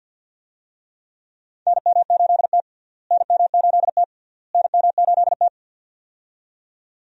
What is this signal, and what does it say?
Morse code, heard as a keyed tone, about 49 wpm, 700 Hz. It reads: NM8T NM8T NM8T
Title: USA Call Signs - 4 Characters - Repeated 3x - 50wpm